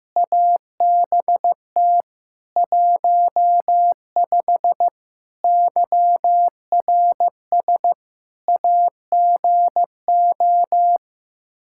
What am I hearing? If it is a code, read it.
Morse code, heard as a keyed tone, about 15 wpm, 700 Hz. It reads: ABT 15 YRS AGO
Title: K1USN SST QSO Practice 15wpm